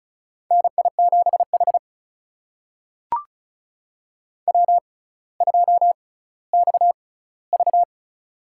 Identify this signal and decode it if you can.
Morse code, heard as a keyed tone, about 36 wpm, spaced as for 18 wpm, 700 Hz. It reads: NI7H W2XV
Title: USA Call Signs - 4 Characters - 6x Character Spacing - 35wpm